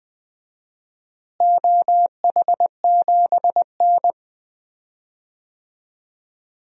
Morse code, heard as a keyed tone, about 21 wpm, 700 Hz. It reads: OH7N
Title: DXCC-100 Call Signs - 4 Characters - Repeated 3x - 20wpm